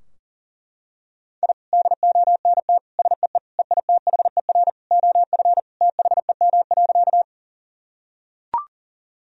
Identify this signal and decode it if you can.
Morse code, heard as a keyed tone, about 37 wpm, 700 Hz. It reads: I DONT SEE EITHER OF THEM.
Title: Sentences from Top 1000 Words - Review Part 4 - 40wpm